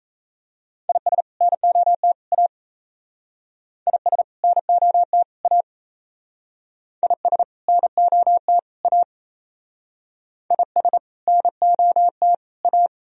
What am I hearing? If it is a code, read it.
Morse code, heard as a keyed tone, about 34 wpm, 700 Hz. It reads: IS NOT A IS NOT A IS NOT A IS NOT A
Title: Speed-Racing - Sets of 3 Words 28wpm